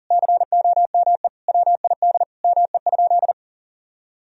Morse code, heard as a keyed tone, about 37 wpm, 700 Hz. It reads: COME WID ME?
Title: Sentences from Top 200 Words and Abbreviations 40wpm